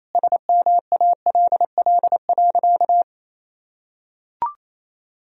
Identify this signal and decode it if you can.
Morse code, heard as a keyed tone, about 28 wpm, 700 Hz. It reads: SMALL.
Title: Sentences from Top 1000 Words - Review Part 1 - 28wpm